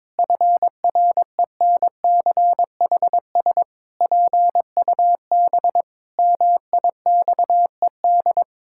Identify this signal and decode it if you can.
Morse code, heard as a keyed tone, about 22 wpm, 700 Hz. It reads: FRENCHS PUB MIXED
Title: Pangram Challenge v2 - 22wpm